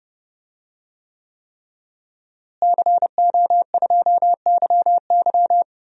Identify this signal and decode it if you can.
Morse code, heard as a keyed tone, about 32 wpm, 700 Hz. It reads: CO2YY
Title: DXCC-100 Call Signs - 5 Characters - Repeated 3x - 30wpm